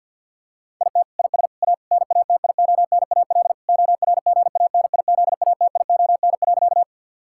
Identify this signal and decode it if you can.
Morse code, heard as a keyed tone, about 50 wpm, 700 Hz. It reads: IT IS A NATIONAL ORGANIZATION.
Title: Sentences from Top 1000 Words - Review Part 5 - 50wpm